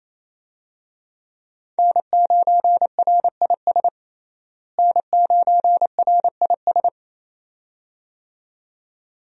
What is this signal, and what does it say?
Morse code, heard as a keyed tone, about 28 wpm, 700 Hz. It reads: N9RIS N9RIS
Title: USA Call Signs - 5 Characters - Repeated 2x - 28wpm